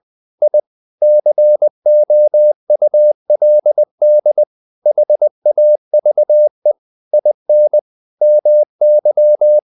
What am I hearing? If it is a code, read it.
Morse code, heard as a keyed tone, about 20 wpm, 600 Hz. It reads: I COULD HAVE IN MY
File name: the old man and the sea part 2 20wpm